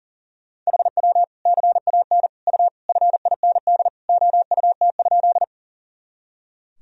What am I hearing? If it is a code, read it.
Morse code, heard as a keyed tone, about 40 wpm, 700 Hz. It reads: HW CAN U FIND OUT?
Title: Sentences from Top 200 Words and Abbreviations 40wpm